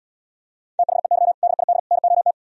Morse code, heard as a keyed tone, about 77 wpm, 700 Hz. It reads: THEY DID NOT